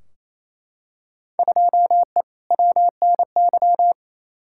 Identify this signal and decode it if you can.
Morse code, heard as a keyed tone, about 28 wpm, 700 Hz. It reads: 2E WNY
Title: Field Day Exchanges Encore - 28wpm